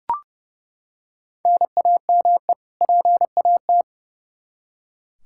Morse code, heard as a keyed tone, about 30 wpm, 700 Hz. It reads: NAME PAT
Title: Practice QSO - Exchange 1 - Beginning - 30wpm